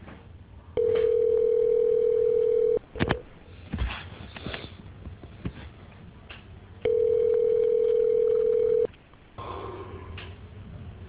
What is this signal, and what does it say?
Ringback tone (North America), 440 + 480 Hz.